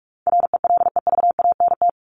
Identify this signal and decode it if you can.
Morse code, heard as a keyed tone, about 44 wpm, 700 Hz. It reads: RELEVANT
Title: Top 1001-1250 Words 45wpm